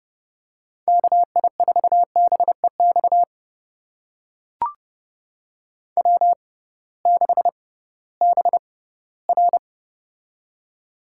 Morse code, heard as a keyed tone, about 30 wpm, spaced as for 17 wpm, 700 Hz. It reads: KI4BEX W6BR
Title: USA Call Signs - 6x Character Spacing - 30wpm